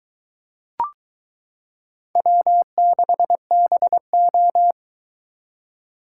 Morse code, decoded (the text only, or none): W6BO